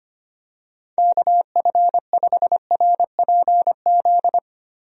Morse code, heard as a keyed tone, about 25 wpm, 700 Hz. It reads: KF5RPZ
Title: USA Call Signs - 6 Characters - Repeated 2x - 25wpm